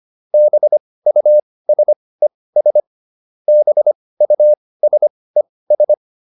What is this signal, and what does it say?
Morse code, heard as a keyed tone, about 25 wpm, spaced as for 18 wpm, 600 Hz. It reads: BUSES BUSES